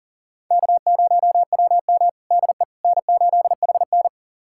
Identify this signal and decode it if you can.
Morse code, heard as a keyed tone, about 40 wpm, 700 Hz. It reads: K0WM DE N8HN
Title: USA Callsign DE Callsign - 4 Characters - 40wpm